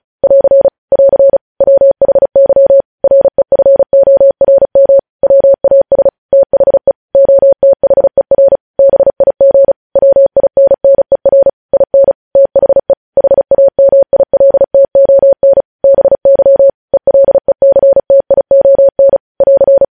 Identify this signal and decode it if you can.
Morse code, heard as a keyed tone, about 36 wpm, 550 Hz. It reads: <AR> <AR> WHY REFORM WAS THE OTHER BIG WINNER IN THE HAMILTON BY ELECTION <AR>
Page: News Headlines in Morse Code at 35 WPM